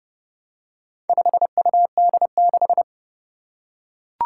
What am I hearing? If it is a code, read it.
Morse code, heard as a keyed tone, about 28 wpm, 700 Hz. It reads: 5UD6
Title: Four Challenging Letter-Number Combinations 30wpm